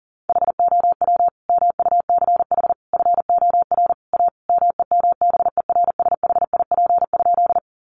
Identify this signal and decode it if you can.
Morse code, heard as a keyed tone, about 40 wpm, 700 Hz. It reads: HOW MUCH FOR A MEMBERSHIP?
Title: Sentences from Top 1400 Words 40wpm